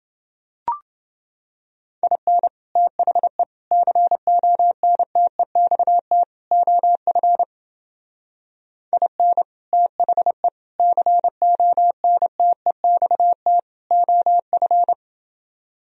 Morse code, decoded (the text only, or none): IN THE CONTEXT OF IN THE CONTEXT OF